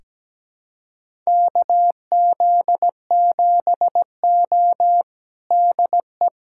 Morse code, heard as a keyed tone, about 17 wpm, 700 Hz. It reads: KZ7O DE
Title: USA Callsign DE Callsign - 4 Characters - 17wpm